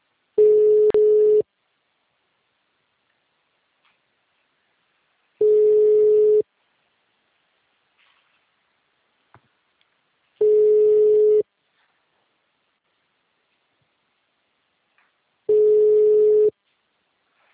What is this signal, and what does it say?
Ringback tone (Europe, 425 Hz).
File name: out- -2017- - - .19364